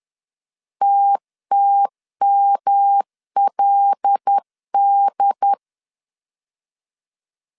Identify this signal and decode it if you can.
Morse code, heard as a keyed tone, about 10 wpm, 800 Hz. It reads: TTMLD